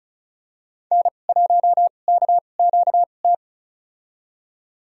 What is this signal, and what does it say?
Morse code, heard as a keyed tone, about 34 wpm, spaced as for 25 wpm, 700 Hz. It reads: N1KQT
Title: USA Call Signs - 5 Characters - 2x Character Spacing - 35wpm